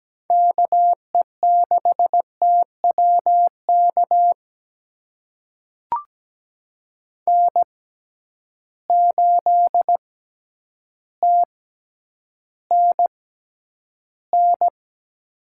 Morse code, heard as a keyed tone, about 17 wpm, spaced as for 9 wpm, 700 Hz. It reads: KE6TWK N8TNN
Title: USA Call Signs - 6x Character Spacing - 17wpm